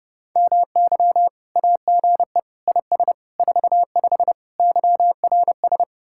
Morse code, heard as a keyed tone, about 36 wpm, 700 Hz. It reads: MY AGE IS 45 YRS
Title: Speed-Racing Common QSO Phrases Encore - 30wpm